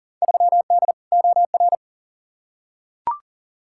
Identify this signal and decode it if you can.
Morse code, heard as a keyed tone, about 40 wpm, 700 Hz. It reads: 3D OR
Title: Field Day Exchanges Encore - 40wpm